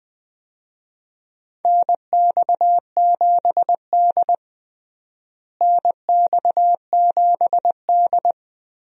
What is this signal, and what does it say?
Morse code, heard as a keyed tone, about 20 wpm, 700 Hz. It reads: NX7D NX7D
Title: USA Call Signs - 4 Characters - Repeated 2x - 20wpm